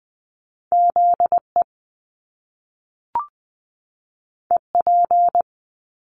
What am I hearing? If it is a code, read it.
Morse code, heard as a keyed tone, about 20 wpm, 700 Hz. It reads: ZE EP
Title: Top Two Letter Combinations in Words 20wpm